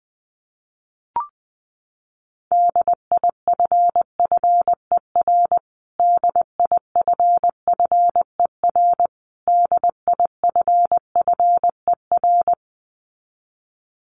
Morse code, heard as a keyed tone, about 19 wpm, 700 Hz. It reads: DIFFER DIFFER DIFFER
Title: Top 500 Words Repeated 20wpm